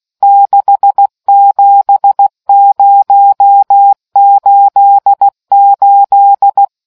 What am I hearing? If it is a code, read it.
Morse code, heard as a keyed tone, about 16 wpm, 800 Hz. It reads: 67088